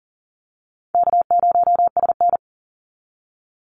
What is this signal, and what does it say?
Morse code, heard as a keyed tone, about 40 wpm, 700 Hz. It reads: K0SN
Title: Call Signs Repeated 2x - 40wpm